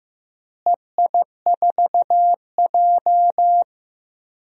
Morse code, heard as a keyed tone, about 15 wpm, 700 Hz. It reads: EI4J
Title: DXCC-100 Call Signs - 4 Characters - Repeated 3x - 15wpm